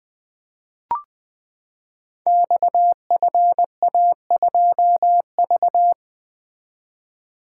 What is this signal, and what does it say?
Morse code, heard as a keyed tone, about 20 wpm, 700 Hz. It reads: XFA2V
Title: Code Groups - 5 Letter:Numbers - 20wpm